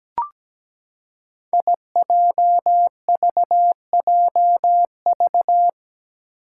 Morse code, heard as a keyed tone, about 17 wpm, 700 Hz. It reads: IJVJV
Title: Five Challenging Letter-Number Combinations 17wpm